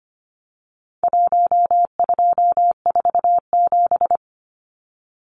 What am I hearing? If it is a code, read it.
Morse code, heard as a keyed tone, about 25 wpm, 700 Hz. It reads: 1247
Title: Numbers - Contest Style - 4-Digit 25wpm